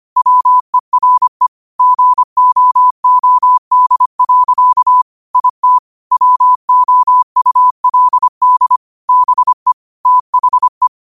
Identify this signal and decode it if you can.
Morse code, heard as a keyed tone, about 25 wpm, 1000 Hz. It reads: WERE GOOD. IT WOULD BE THE
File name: qotdfiles / 25wpm